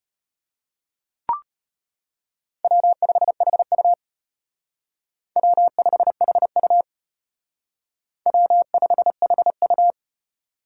Morse code, heard as a keyed tone, about 29 wpm, 700 Hz. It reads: W5HU W5HU W5HU
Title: Speed Racing - USA 1x2 Call Signs 25wpm